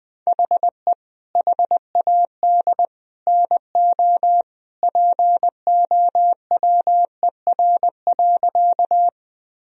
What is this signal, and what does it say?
Morse code, heard as a keyed tone, about 20 wpm, 700 Hz. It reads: HE HAD NO POWER.
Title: Sentences from Top 1000 Words - Review Part 2 - 20wpm